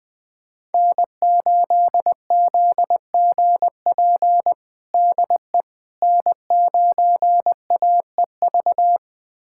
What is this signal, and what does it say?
Morse code, heard as a keyed tone, about 20 wpm, 700 Hz. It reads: N8ZGP DE N9AEV
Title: USA Callsign DE Callsign - 5 Characters - 20wpm